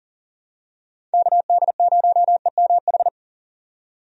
Morse code, heard as a keyed tone, about 40 wpm, 700 Hz. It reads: KD0EMH
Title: USA Call Signs - 6 Characters - 40wpm